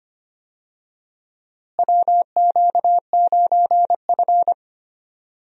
Morse code, heard as a keyed tone, about 25 wpm, 700 Hz. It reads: WQ9F
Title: USA Call Signs - 4 Characters - Repeated 2x - 25wpm